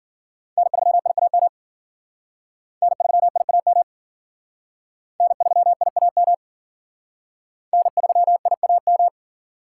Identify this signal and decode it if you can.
Morse code, heard as a keyed tone, about 54 wpm, 700 Hz. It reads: N3IAM N3IAM N3IAM N3IAM